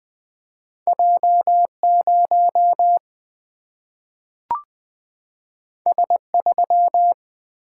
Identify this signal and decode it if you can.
Morse code, heard as a keyed tone, about 20 wpm, 700 Hz. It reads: J 0 S 3
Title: Single-Letter Single-Number 20wpm